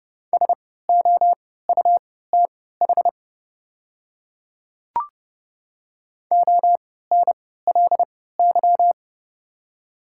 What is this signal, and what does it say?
Morse code, heard as a keyed tone, about 29 wpm, spaced as for 17 wpm, 700 Hz. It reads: SOUTH ONLY
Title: 3x Character Spacing - Top 500 Words - 30wpm